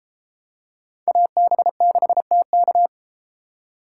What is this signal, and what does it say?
Morse code, heard as a keyed tone, about 30 wpm, 700 Hz. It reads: AB6TK